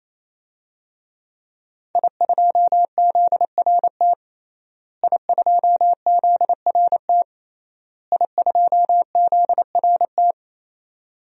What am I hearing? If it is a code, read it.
Morse code, heard as a keyed tone, about 28 wpm, 700 Hz. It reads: I2ZRT I2ZRT I2ZRT
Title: DXCC-100 Call Signs - 5 Characters - Repeated 3x - 28wpm